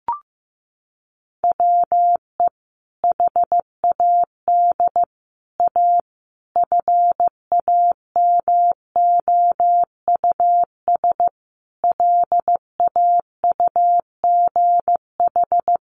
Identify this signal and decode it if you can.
Morse code, heard as a keyed tone, about 15 wpm, 700 Hz. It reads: WE HAD A FAMOUS LAUGH
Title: Sentences from Top 1500 Words 15wpm